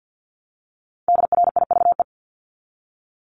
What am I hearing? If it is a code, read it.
Morse code, heard as a keyed tone, about 50 wpm, 700 Hz. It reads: DRIVE
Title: Top 500 Words 50wpm